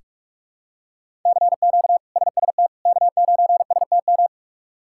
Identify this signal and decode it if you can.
Morse code, heard as a keyed tone, about 46 wpm, 700 Hz. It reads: CQ SST K9STM
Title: K1USN SST QSO Practice - 45wpm